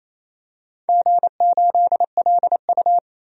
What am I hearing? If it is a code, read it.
Morse code, heard as a keyed tone, about 28 wpm, 700 Hz. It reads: G8LU
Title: DXCC-100 Call Signs - 4 Characters - 28wpm